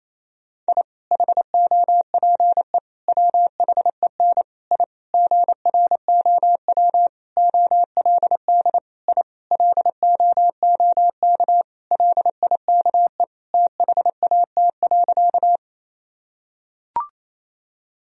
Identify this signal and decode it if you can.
Morse code, heard as a keyed tone, about 28 wpm, 700 Hz. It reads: I HOPE WHEN I GROW OLD I LOOK LIKE THAT.
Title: Sentences from Top 1000 Words - Review Part 1 - 28wpm